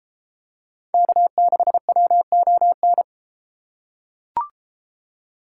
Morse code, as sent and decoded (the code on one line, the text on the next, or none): -.- -.... .-- --- -.
K6WON